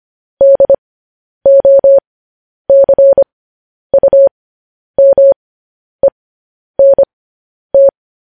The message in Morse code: DOCUMENT

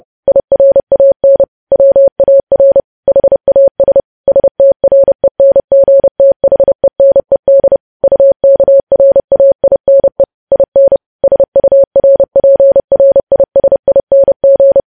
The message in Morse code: IRAN WAR HAS STRENGTHENED UKRAINE IN SURPRISING

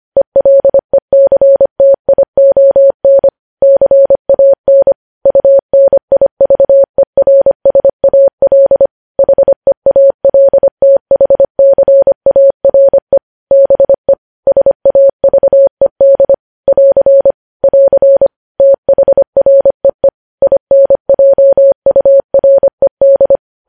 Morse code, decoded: ELECTION CAN UNIVERSAL HEALTHCARE BE SAVED <AR> <AR> THREE INJURED

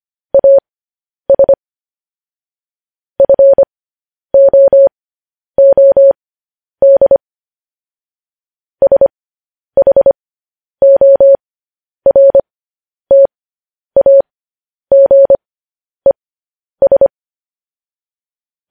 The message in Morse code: AS FOOD SHORTAGES

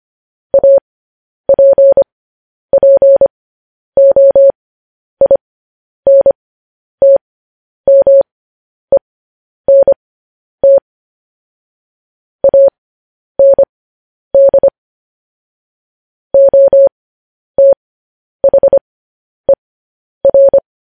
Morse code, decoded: APPOINTMENT AND OTHER

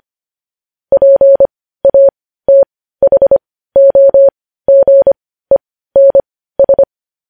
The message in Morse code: PATHOGENS

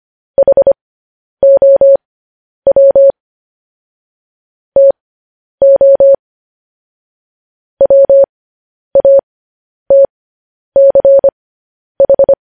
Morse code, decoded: HOW TO WATCH

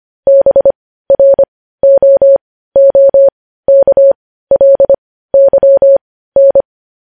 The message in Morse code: BROOKLYN

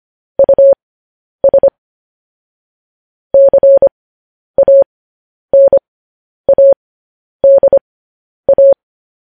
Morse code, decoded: US CANADA

